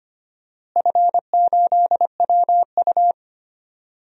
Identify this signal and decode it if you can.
Morse code, heard as a keyed tone, about 25 wpm, 700 Hz. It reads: F8WU
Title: DXCC-100 Call Signs - 4 Characters - Repeated 3x - 25wpm